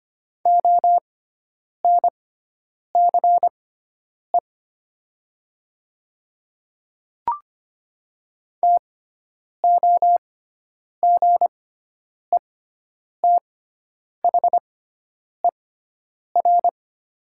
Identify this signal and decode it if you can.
Morse code, heard as a keyed tone, about 25 wpm, spaced as for 9 wpm, 700 Hz. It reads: ONCE TOGETHER
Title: 6x Character Spacing - Top 500 Words - 25wpm